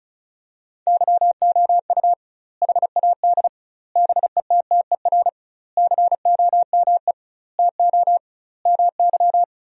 Morse code, heard as a keyed tone, about 36 wpm, 700 Hz. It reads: YOU HAD BETTER COME TO MY
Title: 2x Word-Spacing - Sentences from Top 400 Words - 35wpm